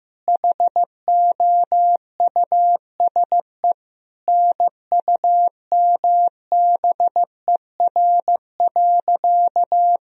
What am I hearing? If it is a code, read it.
Morse code, heard as a keyed tone, about 15 wpm, 700 Hz. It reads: HOUSE NUMBER.
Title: Sentences from Top 1000 Words - Review Part 1 - 15wpm